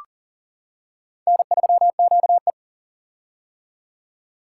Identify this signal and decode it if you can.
Morse code, heard as a keyed tone, about 40 wpm, 700 Hz. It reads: N3QE
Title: K1USN SST QSO Practice - 40wpm